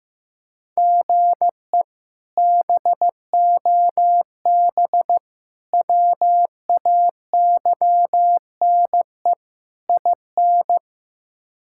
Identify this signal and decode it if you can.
Morse code, heard as a keyed tone, about 15 wpm, 700 Hz. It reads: GE BOB WAYNE IN
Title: K1USN SST QSO Practice - 15wpm